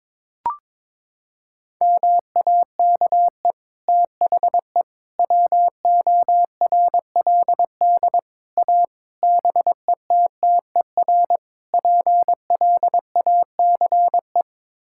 Morse code, heard as a keyed tone, about 22 wpm, 700 Hz. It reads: MAKE THE WORLD A BETTER PLACE
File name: practice / 22wpm / Sets of 6 Words - 22wpm